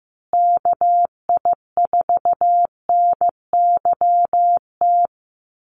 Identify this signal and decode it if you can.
Morse code, heard as a keyed tone, about 15 wpm, 700 Hz. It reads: KI4NYT
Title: USA Call Signs Repeated 2x 15wpm